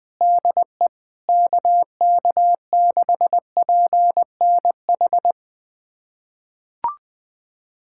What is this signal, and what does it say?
Morse code, heard as a keyed tone, about 21 wpm, 700 Hz. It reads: DE KK6PNH
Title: USA Callsign DE Callsign - 6 Characters - 20wpm